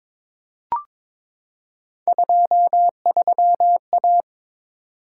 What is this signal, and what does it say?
Morse code, heard as a keyed tone, about 22 wpm, 700 Hz. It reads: 23A
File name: practice / 22wpm / Three Letter-Number Combinations - 22wpm